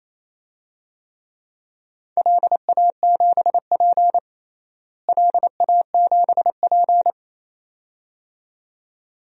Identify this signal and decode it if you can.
Morse code, heard as a keyed tone, about 28 wpm, 700 Hz. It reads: LA7P LA7P
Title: DXCC-100 Call Signs - 4 Characters - Repeated 2x - 28wpm